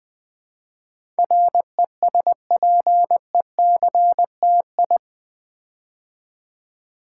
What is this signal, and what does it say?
Morse code, heard as a keyed tone, about 20 wpm, 700 Hz. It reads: RESPECTI
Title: Build A Word - 4 Letters At A Time - 12-Letter Words - 20wpm